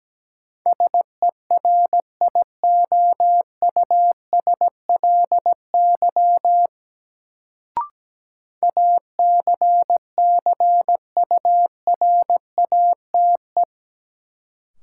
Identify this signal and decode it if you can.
Morse code, heard as a keyed tone, about 17 wpm, 700 Hz. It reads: SERIOUSLY ACCURATE
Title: Sentences from Top 1200 Words - New Word Drill - 17wpm